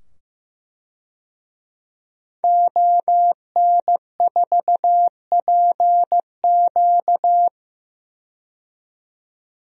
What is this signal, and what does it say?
Morse code, heard as a keyed tone, about 15 wpm, 700 Hz. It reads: ON4PQ
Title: DXCC-100 Call Signs - 5 Characters - 15wpm